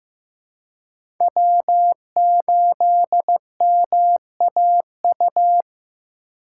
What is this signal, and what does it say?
Morse code, heard as a keyed tone, about 15 wpm, 700 Hz. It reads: W8MAU
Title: USA Call Signs - 5 Characters - 4x Character Spacing - 15wpm